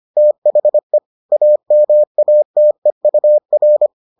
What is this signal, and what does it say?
Morse code, heard as a keyed tone, about 25 wpm, 600 Hz. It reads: THE AMATEUR